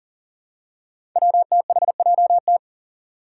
Morse code, heard as a keyed tone, about 40 wpm, 700 Hz. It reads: WTHJT
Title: Five Challenging Letter-Number Combinations 40wpm